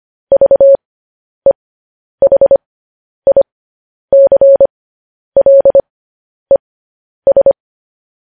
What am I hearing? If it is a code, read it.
Morse code, heard as a keyed tone, about 25 wpm, spaced as for 10 wpm, 550 Hz. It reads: VEHICLES
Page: news headlines in morse code at 10 wpm